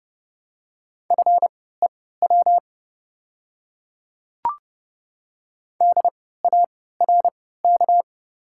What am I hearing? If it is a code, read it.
Morse code, heard as a keyed tone, about 30 wpm, spaced as for 17 wpm, 700 Hz. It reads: FEW DARK
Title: 3x Character Spacing - Top 500 Words - 30wpm